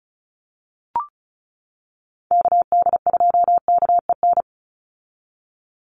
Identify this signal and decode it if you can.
Morse code, heard as a keyed tone, about 34 wpm, 700 Hz. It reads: KD2KEN
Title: USA Call Signs 35wpm